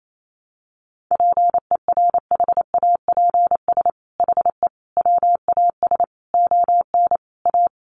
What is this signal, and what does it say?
Morse code, heard as a keyed tone, about 28 wpm, 700 Hz. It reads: PERHAPS HE WAS ON A
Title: Sentences from Top 1100 Words 28wpm